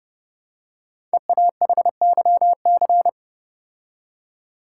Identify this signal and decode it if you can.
Morse code, heard as a keyed tone, about 30 wpm, 700 Hz. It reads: EAHYC
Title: Code Groups - 5 Letters - 30wpm